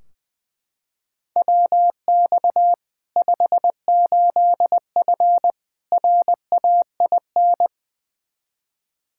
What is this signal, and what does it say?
Morse code, heard as a keyed tone, about 20 wpm, 700 Hz. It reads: WX 58F RAIN
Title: Practice QSO - Exchange 2 - Info - 20wpm